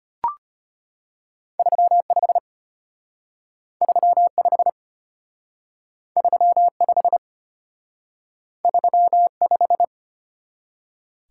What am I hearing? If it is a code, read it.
Morse code, heard as a keyed tone, about 32 wpm, 700 Hz. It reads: 35 35 35 35